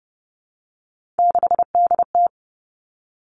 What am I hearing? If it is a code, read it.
Morse code, heard as a keyed tone, about 30 wpm, 700 Hz. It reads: 6DT
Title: Three Challenging Letter-Number Combinations 30wpm